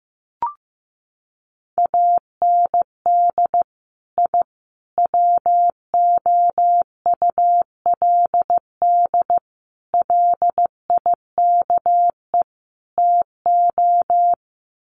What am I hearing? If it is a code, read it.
Morse code, heard as a keyed tone, about 15 wpm, 700 Hz. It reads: AND I WOULD LIKE TO
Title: Sets of 5 Words Encore 15wpm